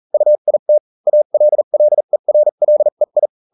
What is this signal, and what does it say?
Morse code, heard as a keyed tone, about 40 wpm, spaced as for 35 wpm, 600 Hz. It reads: UIT ALLERLEI